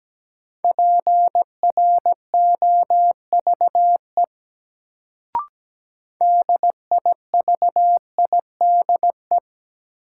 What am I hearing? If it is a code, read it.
Morse code, heard as a keyed tone, about 17 wpm, 700 Hz. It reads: PROVE DIVIDE
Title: Sentences from Top 1600 Words - New Word Drill Alt - 17wpm